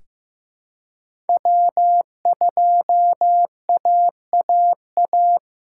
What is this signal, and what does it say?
Morse code, heard as a keyed tone, about 15 wpm, 700 Hz. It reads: W2AAA
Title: USA Call Signs - 5 Characters - 6x Character Spacing - 15wpm